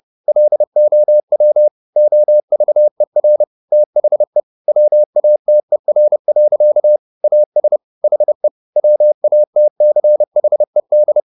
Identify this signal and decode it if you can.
Morse code, heard as a keyed tone, about 30 wpm, 600 Hz. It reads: LOW OVER THE WATER. AS HE WATCHED
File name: the old man and the sea / the old man and the sea 1 30wpm